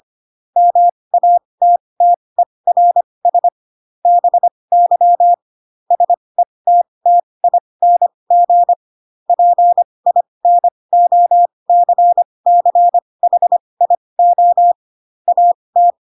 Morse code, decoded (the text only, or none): MATTERS BY SETTING PINOCCHIO AT